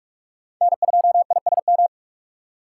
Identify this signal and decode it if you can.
Morse code, heard as a keyed tone, about 40 wpm, 700 Hz. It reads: N2ISM